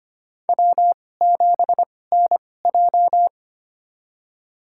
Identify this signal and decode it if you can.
Morse code, heard as a keyed tone, about 25 wpm, spaced as for 18 wpm, 700 Hz. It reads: W7NJ
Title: USA Call Signs - 4 Characters - 2x Character Spacing - 25wpm